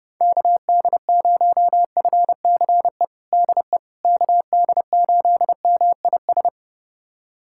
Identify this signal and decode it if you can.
Morse code, heard as a keyed tone, about 32 wpm, 700 Hz. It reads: KD0FCE DE KD8MIS
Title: USA Callsign DE Callsign 30wpm